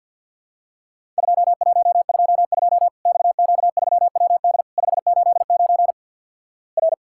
Morse code, heard as a keyed tone, about 50 wpm, 700 Hz. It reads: 2122 XQ3WD 589 R
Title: Worked All Europe DX Contest - QTC Exchange - 50wpm